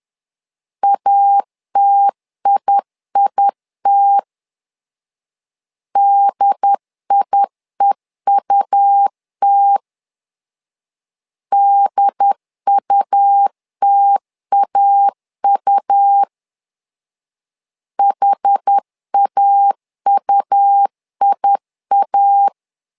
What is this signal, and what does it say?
Morse code, heard as a keyed tone, about 10 wpm, 800 Hz. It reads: ATIIT DIEUT DUTAU HAUIA
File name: LEZIONI / 2 UHDN / 2 UHD 10wpm 1826